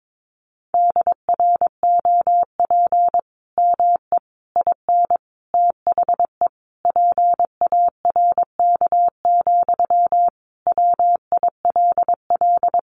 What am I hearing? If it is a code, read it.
Morse code, heard as a keyed tone, about 22 wpm, 700 Hz. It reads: DROP ME IN THE PARK, WILL
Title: Sentences from Top 1400 Words 22wpm